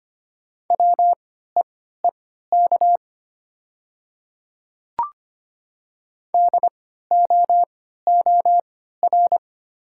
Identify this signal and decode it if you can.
Morse code, heard as a keyed tone, about 25 wpm, spaced as for 14 wpm, 700 Hz. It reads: WEEK DOOR
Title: 3x Character Spacing - Top 500 Words - 25wpm